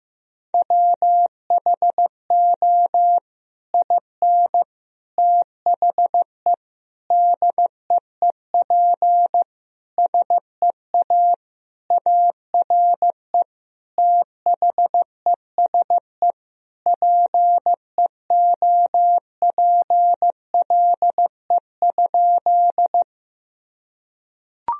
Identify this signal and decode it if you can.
Morse code, heard as a keyed tone, about 15 wpm, 700 Hz. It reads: WHO IN THE DEEP SEA ARE THESE PEOPLE?
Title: Sentences from Top 1000 Words - Review Part 3 - 15wpm